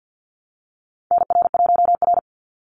Morse code, heard as a keyed tone, about 50 wpm, 700 Hz. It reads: NR1R
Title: Call Signs Repeated 3x - 50wpm